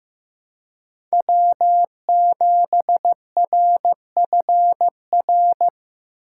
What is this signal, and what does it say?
Morse code, heard as a keyed tone, about 15 wpm, 700 Hz. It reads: W7RFR
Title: USA Call Signs - 5 Characters - 6x Character Spacing - 15wpm